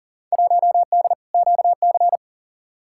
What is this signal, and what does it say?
Morse code, heard as a keyed tone, about 40 wpm, 700 Hz. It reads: 1D QC
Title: Field Day Exchanges - 40wpm